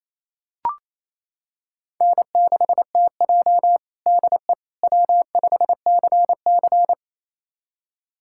Morse code, heard as a keyed tone, about 28 wpm, 700 Hz. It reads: N6TJ DE W5CC
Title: USA Callsign DE Callsign - 4 Characters - 28wpm